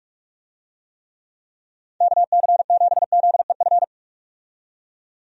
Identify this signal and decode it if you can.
Morse code, heard as a keyed tone, about 43 wpm, 700 Hz. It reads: KC7ZEF
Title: USA Call Signs - 6 Characters - 45wpm